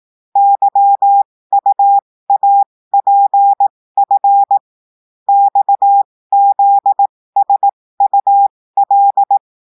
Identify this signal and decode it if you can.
Morse code, heard as a keyed tone, about 18 wpm, spaced as for 15 wpm, 800 Hz. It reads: YUAPF XZSUL
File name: cours-du-lundi files / cours-1580-texte-chiffre-15-mpm